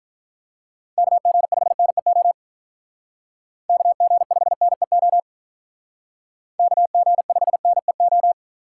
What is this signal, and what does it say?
Morse code, heard as a keyed tone, about 43 wpm, 700 Hz. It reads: KG5NEO KG5NEO KG5NEO
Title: Speed Racing - USA 2x3 Call Signs 35wpm